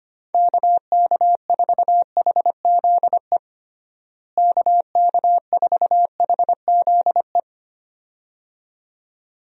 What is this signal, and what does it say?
Morse code, heard as a keyed tone, about 25 wpm, 700 Hz. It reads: KK4HZE KK4HZE
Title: USA Call Signs - 6 Characters - Repeated 2x - 25wpm